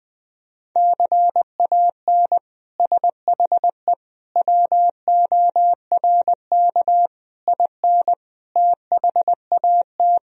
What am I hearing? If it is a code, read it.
Morse code, heard as a keyed tone, about 21 wpm, 700 Hz. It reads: CAN SHE WORK IN THAT
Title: Sentences from Top 1000 Words - Review Part 4 - 20wpm